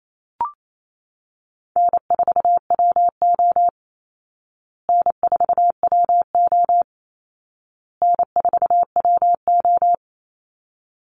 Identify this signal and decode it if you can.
Morse code, heard as a keyed tone, about 28 wpm, 700 Hz. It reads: N4WO N4WO N4WO
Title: Easy Call Signs Repeated 3x - 28wpm